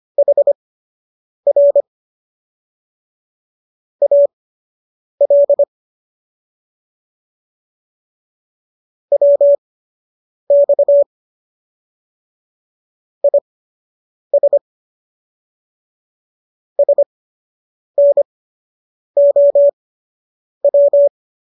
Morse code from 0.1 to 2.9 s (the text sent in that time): HR